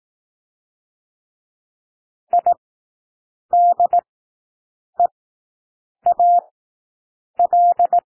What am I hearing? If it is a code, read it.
Morse code, heard as a keyed tone, about 18 wpm, spaced as for 7 wpm, 700 Hz. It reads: IDEAL
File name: SOUND WORDS 18 wpm